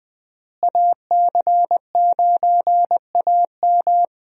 Morse code, heard as a keyed tone, about 21 wpm, 700 Hz. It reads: AC9AM
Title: USA Call Signs - 5 Characters - Repeated 2x - 20wpm